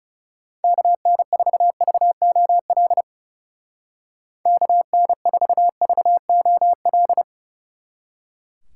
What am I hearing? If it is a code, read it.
Morse code, heard as a keyed tone, about 39 wpm, 700 Hz. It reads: KN4VOL KN4VOL